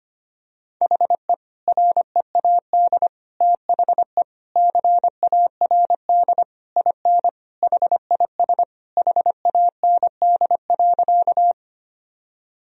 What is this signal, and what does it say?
Morse code, heard as a keyed tone, about 25 wpm, 700 Hz. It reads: HE READ THE CARD IN HIS HAND.
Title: Sentences from Top 1000 Words - Review Part 4 - 25wpm